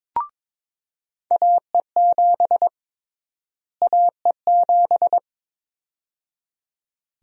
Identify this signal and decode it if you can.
Morse code, heard as a keyed tone, about 22 wpm, 700 Hz. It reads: AE7 AE7
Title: USA Call Sign Prefixes - Repeated 2x - 22wpm